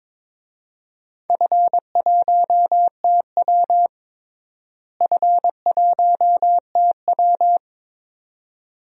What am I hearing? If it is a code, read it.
Morse code, heard as a keyed tone, about 22 wpm, 700 Hz. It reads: F1TW F1TW
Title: DXCC-100 Call Signs - 4 Characters - Repeated 2x - 22wpm